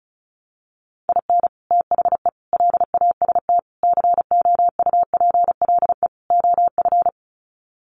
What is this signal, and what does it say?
Morse code, heard as a keyed tone, about 34 wpm, 700 Hz. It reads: IN THE LAST COUPLE OF
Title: Sets of 5 Words 35wpm